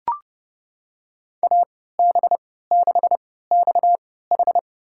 Morse code, heard as a keyed tone, about 30 wpm, spaced as for 17 wpm, 700 Hz. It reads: AB6XH
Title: USA Call Signs - 5 Characters - 3x Character Spacing - 30wpm